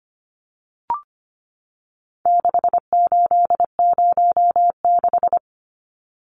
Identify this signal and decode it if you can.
Morse code, heard as a keyed tone, about 25 wpm, 700 Hz. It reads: 6806
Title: Numbers - 4-Digit 25wpm